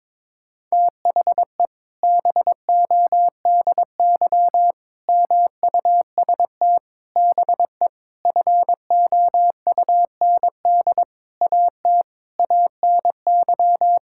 Morse code, heard as a keyed tone, about 22 wpm, 700 Hz. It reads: THE BODY MUST BE FOUND AT ANY
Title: Sentences from Top 1000 Words - Review Part 5 - 22wpm